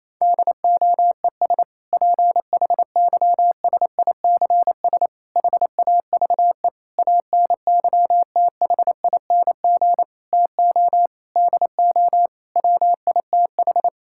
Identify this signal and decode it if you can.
Morse code, heard as a keyed tone, about 27 wpm, 700 Hz. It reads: DOES PHYSICS HAVE ANYTHING TO DO WITH
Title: Sentences from Top 1000 Words - Review Part 5 - 28wpm